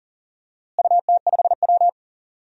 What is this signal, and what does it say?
Morse code, heard as a keyed tone, about 37 wpm, 700 Hz. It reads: UT5W
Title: DXCC-100 Call Signs - 4 Characters - Repeated 2x - 40wpm